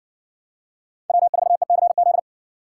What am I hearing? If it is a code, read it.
Morse code, heard as a keyed tone, about 58 wpm, 700 Hz. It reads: W4ECZ